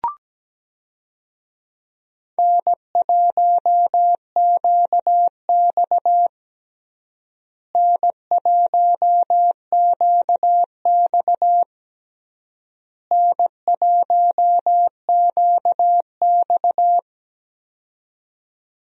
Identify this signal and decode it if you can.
Morse code, heard as a keyed tone, about 17 wpm, 700 Hz. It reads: N1QX N1QX N1QX
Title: USA Call Signs - 4 Characters - Repeated 3x - 17wpm